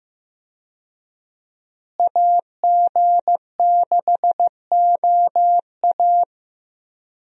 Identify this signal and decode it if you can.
Morse code, heard as a keyed tone, about 15 wpm, 700 Hz. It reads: AG6OA
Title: USA Call Signs - 5 Characters - Repeated 2x - 15wpm